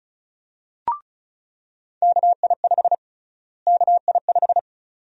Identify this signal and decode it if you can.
Morse code, heard as a keyed tone, about 36 wpm, 700 Hz. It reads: KI5 KI5
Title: USA Call Sign Prefixes - Repeated 2x - 35wpm